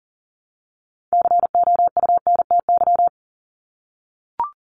Morse code, heard as a keyed tone, about 40 wpm, 700 Hz. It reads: COUNTY
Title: Top 1001-1250 Words 40wpm